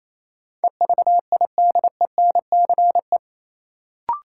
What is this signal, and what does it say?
Morse code, heard as a keyed tone, about 28 wpm, 700 Hz. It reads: EVIDENCE
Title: Sentences from Top 1300 Words - New Word Drill Alt - 28wpm